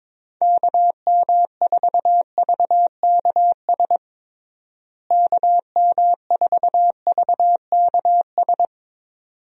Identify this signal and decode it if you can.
Morse code, heard as a keyed tone, about 22 wpm, 700 Hz. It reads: KM4VKS KM4VKS
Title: USA Call Signs - 6 Characters - Repeated 2x - 22wpm